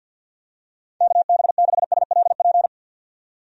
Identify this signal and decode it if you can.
Morse code, heard as a keyed tone, about 52 wpm, 700 Hz. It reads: KB6SLP
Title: USA Call Signs - 6 Characters - 50wpm